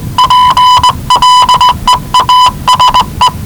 Morse code, heard as a keyed tone, about 19 wpm, 1000 Hz. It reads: PLEASE
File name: WellingtonRadioZlwCloseDownOnHfAndMf / 01-Zlw 4277khz 417.5 qrt until 0925z